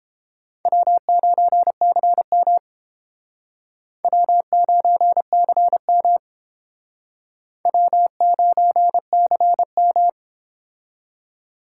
Morse code, heard as a keyed tone, about 31 wpm, 700 Hz. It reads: W9CM W9CM W9CM